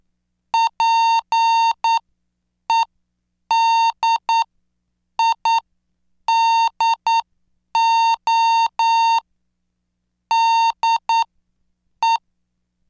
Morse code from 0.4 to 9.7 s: PEDIDO